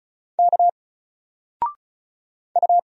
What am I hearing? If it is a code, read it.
Morse code, heard as a keyed tone, about 36 wpm, 700 Hz. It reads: K U
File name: Single-Letter-Number-35wpm